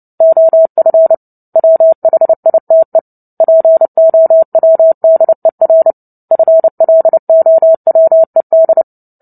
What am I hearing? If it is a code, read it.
Morse code, heard as a keyed tone, about 30 wpm, 650 Hz. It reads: OF WHITE POWDER FLOWED
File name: Book Two / 03-Chapter 3 - 29 WPM